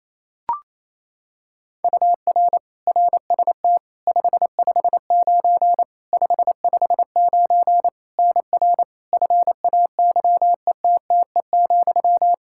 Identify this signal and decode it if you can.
Morse code, heard as a keyed tone, about 28 wpm, 700 Hz. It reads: UR RST 559 559 NR FAYETTE,
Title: Practice QSO - Exchange 1 - Beginning - 28wpm